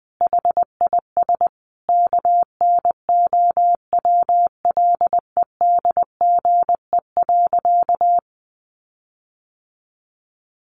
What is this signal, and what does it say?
Morse code, heard as a keyed tone, about 20 wpm, 700 Hz. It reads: HIS KNOWLEDGE.
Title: Sentences from Top 1300 Words 20wpm